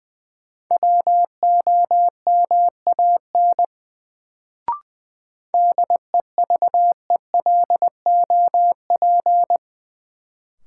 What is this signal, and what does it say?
Morse code, heard as a keyed tone, about 21 wpm, 700 Hz. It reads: WOMAN DEVELOP
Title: Sentences from Top 1000 Words - Review Part 2 - New Word Drill - 20wpm